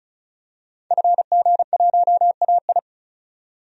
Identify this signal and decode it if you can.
Morse code, heard as a keyed tone, about 34 wpm, 700 Hz. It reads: FG1AI
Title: DXCC-100 Call Signs - 5 Characters - 35wpm